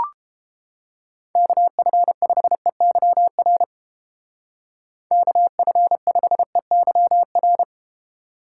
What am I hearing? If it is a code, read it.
Morse code, heard as a keyed tone, about 31 wpm, 700 Hz. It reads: KF5EYR KF5EYR